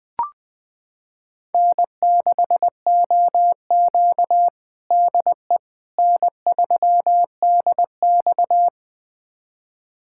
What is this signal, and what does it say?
Morse code, heard as a keyed tone, about 20 wpm, 700 Hz. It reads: N6OQ DE N3DX
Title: USA Callsign DE Callsign - 4 Characters - 20wpm